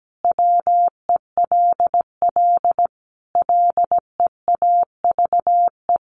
Morse code, heard as a keyed tone, about 17 wpm, 700 Hz. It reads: WELL LEAVE
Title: Sentences from Top 800 Words 17wpm